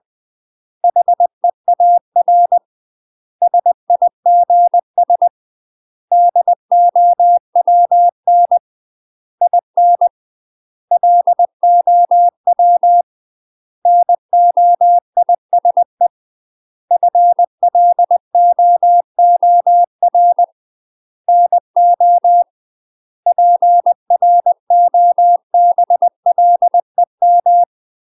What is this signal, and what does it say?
Morse code, heard as a keyed tone, about 20 wpm, 700 Hz. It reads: HEAR SIGS DOWN IN LOW NOISE FLOOR NO PROBLEM
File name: UK MASTER QSOs TRAINING / 20WPM / UK MASTER QSOs 0030